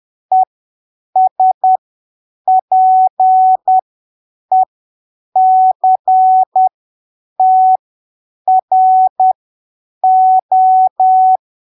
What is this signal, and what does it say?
Morse code, heard as a keyed tone, about 10 wpm, spaced as for 7 wpm, 750 Hz. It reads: ESPECTRO
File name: arquivos / texto / texto 06 10ppm